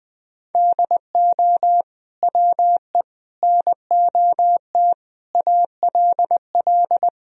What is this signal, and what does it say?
Morse code, heard as a keyed tone, about 20 wpm, 700 Hz. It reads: DO WE NOT ALL
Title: Sentences from Top 1000 Words - Review Part 3 - 20wpm